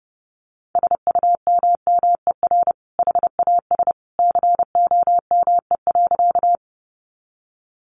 Morse code, heard as a keyed tone, about 30 wpm, 700 Hz. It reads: SUMMER HAS COME.
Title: Sentences from Top 1100 Words 30wpm